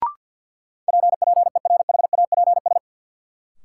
Morse code, heard as a keyed tone, about 51 wpm, 700 Hz. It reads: PPERHAPS
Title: Sentences from Top 1100 Words - New Word Drill - 50wpm